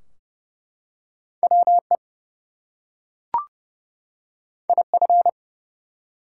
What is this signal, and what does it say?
Morse code, heard as a keyed tone, about 31 wpm, 700 Hz. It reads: WE IF
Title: Two Letter Words - 30wpm